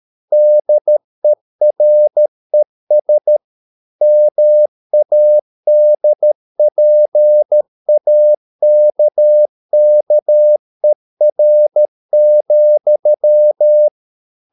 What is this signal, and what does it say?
Morse code, heard as a keyed tone, about 13 wpm, 600 Hz. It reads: DERES MADPAKKER,
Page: Ferie 13 wpm